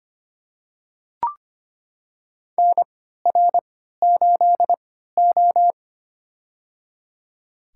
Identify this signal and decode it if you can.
Morse code, heard as a keyed tone, about 25 wpm, spaced as for 14 wpm, 700 Hz. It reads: NR8O
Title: USA Call Signs - 4 Characters - 3x Character Spacing - 25wpm